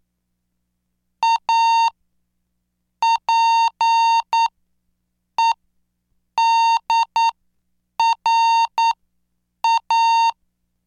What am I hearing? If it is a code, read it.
Morse code, heard as a keyed tone, about 9 wpm, spaced as for 7 wpm, 920 Hz. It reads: A PEDRA